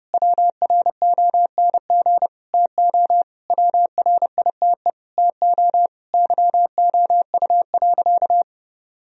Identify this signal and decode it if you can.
Morse code, heard as a keyed tone, about 32 wpm, 700 Hz. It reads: WRONG TO WRITE TO YOU.
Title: Sentences from Top 1000 Words - Review Part 1 - 30wpm